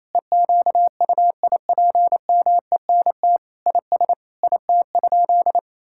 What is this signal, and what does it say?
Morse code, heard as a keyed tone, about 28 wpm, 700 Hz. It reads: EQUIPMENT IS IT?
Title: Sentences from Top 1000 Words - Review Part 5 - 28wpm